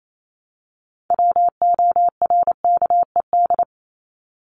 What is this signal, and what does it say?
Morse code, heard as a keyed tone, about 28 wpm, 700 Hz. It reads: WORKED
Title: ED Suffix - Top 100 Words with Base Words 28wpm